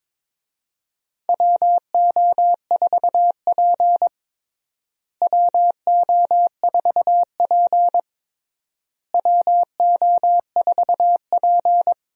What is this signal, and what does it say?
Morse code, heard as a keyed tone, about 22 wpm, 700 Hz. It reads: WO4P WO4P WO4P
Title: USA Call Signs - 4 Characters - Repeated 3x - 22wpm